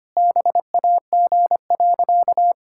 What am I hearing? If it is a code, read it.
Morse code, heard as a keyed tone, about 25 wpm, 700 Hz. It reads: BAG.
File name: practice / 25wpm / Sentences with 3-Word Phrases - 25wpm